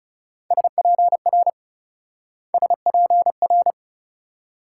Morse code, heard as a keyed tone, about 39 wpm, 700 Hz. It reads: SPR SPR